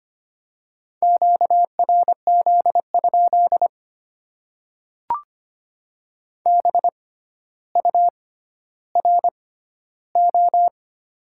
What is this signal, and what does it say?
Morse code, heard as a keyed tone, about 25 wpm, spaced as for 13 wpm, 700 Hz. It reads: QRZ? BURO
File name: Top 90 QSO Elements - 6x Character Spacing - 25wpm